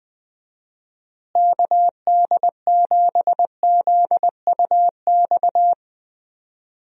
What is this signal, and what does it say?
Morse code, heard as a keyed tone, about 20 wpm, 700 Hz. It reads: KD7ZUX
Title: USA Call Signs - 6 Characters - Repeated 2x - 20wpm